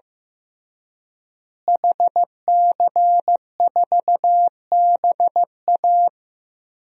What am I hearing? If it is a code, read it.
Morse code, heard as a keyed tone, about 15 wpm, 700 Hz. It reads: HC4BA
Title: DXCC-100 Call Signs - 5 Characters - Repeated 3x - 15wpm